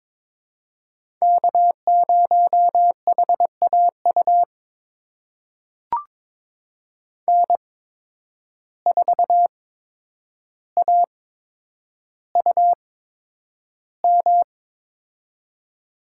Morse code, heard as a keyed tone, about 22 wpm, spaced as for 9 wpm, 700 Hz. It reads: K0HAU N4AUM
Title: USA Call Signs - 5 Characters - 8x Character Spacing - 22wpm